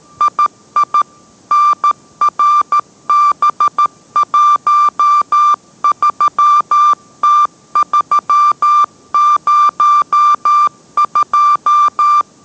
Morse code, heard as a keyed tone, about 13 wpm, 1180 Hz. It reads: II NRB13T302